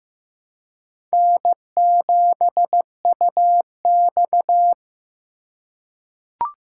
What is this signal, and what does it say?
Morse code, heard as a keyed tone, about 15 wpm, 700 Hz. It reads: N7UX
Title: USA Call Signs - 4 Characters - 4x Character Spacing - 15wpm